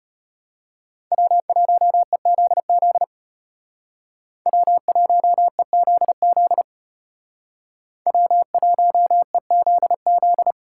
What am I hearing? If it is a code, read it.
Morse code, heard as a keyed tone, about 31 wpm, 700 Hz. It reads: W1EZZ W1EZZ W1EZZ
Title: Speed Racing - USA 1x3 Call Signs 25wpm